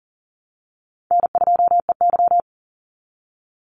Morse code, heard as a keyed tone, about 40 wpm, 700 Hz. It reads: N2EY
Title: Call Signs Repeated 3x - 40wpm